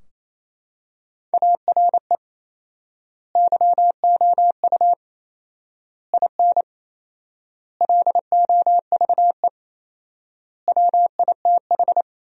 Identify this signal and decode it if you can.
Morse code, heard as a keyed tone, about 28 wpm, 700 Hz. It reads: ARE YOU IN LOVE WITH
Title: 4x Word-Spacing - Sentences from Top 400 Words - 28wpm